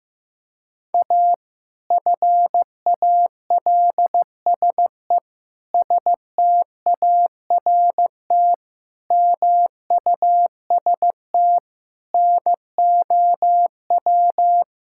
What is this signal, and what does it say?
Morse code, heard as a keyed tone, about 15 wpm, 700 Hz. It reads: A FALSE START MUST NOW
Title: Sentences from Top 1600 Words 15wpm